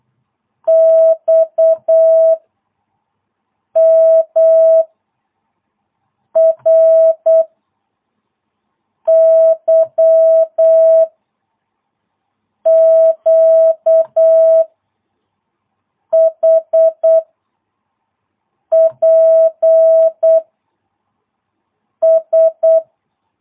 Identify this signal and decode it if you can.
Morse code, heard as a keyed tone, about 8 wpm, 660 Hz. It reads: X M R Y Q H P S